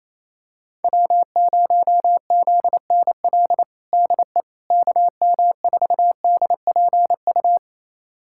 Morse code, heard as a keyed tone, about 28 wpm, 700 Hz. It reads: W0ZNL DE KM4DPU
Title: USA Callsign DE Callsign 28wpm